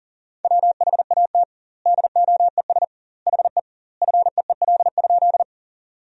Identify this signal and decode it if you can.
Morse code, heard as a keyed tone, about 40 wpm, 700 Hz. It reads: WHAT DOES HE FEEL?
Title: 2x Word-Spacing - Sentences from Top 400 Words - 40wpm